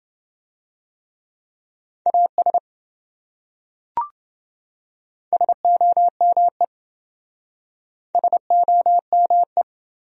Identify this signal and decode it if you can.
Morse code, heard as a keyed tone, about 26 wpm, 700 Hz. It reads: AS SOME SOME